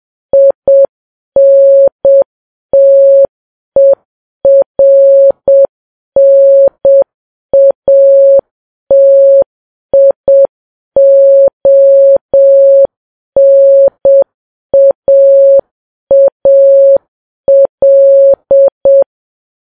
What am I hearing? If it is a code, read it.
Morse code, heard as a keyed tone, about 7 wpm, 550 Hz. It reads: INTERNATIONAAL